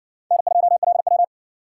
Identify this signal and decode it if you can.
Morse code, heard as a keyed tone, about 60 wpm, 700 Hz. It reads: N2LW